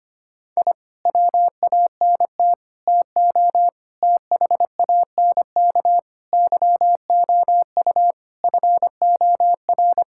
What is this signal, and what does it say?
Morse code, heard as a keyed tone, about 25 wpm, 700 Hz. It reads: I WANT TO THANK YOU FOR
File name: practice / 25wpm / Sentences with 9-Word Phrases - 25wpm